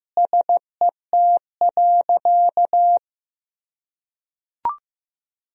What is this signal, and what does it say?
Morse code, heard as a keyed tone, about 15 wpm, 700 Hz. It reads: SET.
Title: Sentences from Top 400 Words and Abbreviations 15wpm